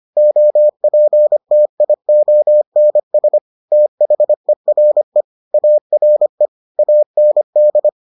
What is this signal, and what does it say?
Morse code, heard as a keyed tone, about 25 wpm, 600 Hz. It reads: OPTIONS THERE ARE AND